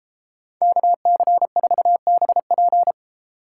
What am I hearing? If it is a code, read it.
Morse code, heard as a keyed tone, about 33 wpm, 700 Hz. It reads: KC4BP